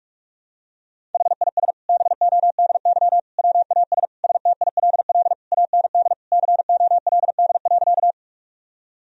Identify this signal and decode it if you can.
Morse code, heard as a keyed tone, about 46 wpm, 700 Hz. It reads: HIS BODY WAS STILL AND COLD.
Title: Sentences from Top 1000 Words - Review Part 2 - 45wpm